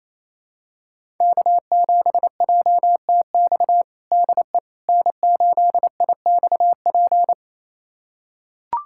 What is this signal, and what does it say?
Morse code, heard as a keyed tone, about 28 wpm, 690 Hz. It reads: K7JTX DE N8IXP
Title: USA Callsign DE Callsign - 5 Characters - 28wpm